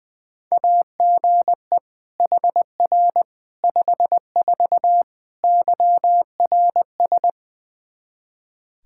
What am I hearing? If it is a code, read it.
Morse code, heard as a keyed tone, about 20 wpm, 700 Hz. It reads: AGE HR 54 YRS
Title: Practice QSO - Exchange 3 - More Info - 20wpm